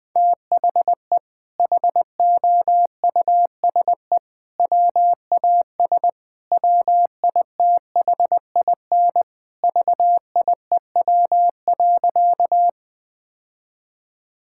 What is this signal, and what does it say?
Morse code, heard as a keyed tone, about 20 wpm, 700 Hz. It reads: THE HOUSE WAS WITHIN VIEW.
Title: Sentences from Top 1000 Words - Review Part 6 - 20wpm